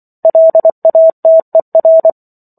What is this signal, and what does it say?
Morse code, heard as a keyed tone, about 24 wpm, 650 Hz. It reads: LATER